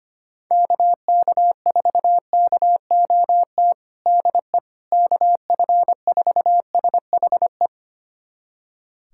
Morse code, heard as a keyed tone, about 25 wpm, 700 Hz. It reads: KK4KOT DE KF4SHE
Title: USA Callsign DE Callsign 25wpm